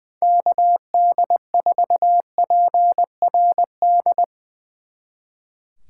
Morse code, heard as a keyed tone, about 20 wpm, 700 Hz. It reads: KD4PRD
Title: Speed Racing - USA 2x3 Call Signs 20wpm